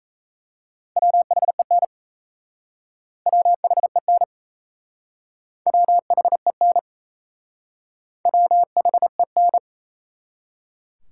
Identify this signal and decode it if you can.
Morse code, heard as a keyed tone, about 36 wpm, 700 Hz. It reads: WHEN WHEN WHEN WHEN